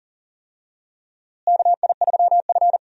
Morse code, heard as a keyed tone, about 40 wpm, 700 Hz. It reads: KI3F
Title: USA Call Signs - 4 Characters - Repeated 2x - 40wpm